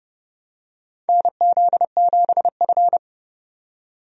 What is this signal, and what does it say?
Morse code, heard as a keyed tone, about 30 wpm, 700 Hz. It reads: NZ7F
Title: USA Call Signs - 4 Characters - 30wpm